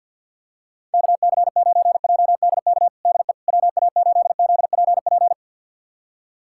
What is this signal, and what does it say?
Morse code, heard as a keyed tone, about 46 wpm, 690 Hz. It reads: KC9JDK DE WA8ZPP
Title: USA Callsign DE Callsign - 6 Characters - 50wpm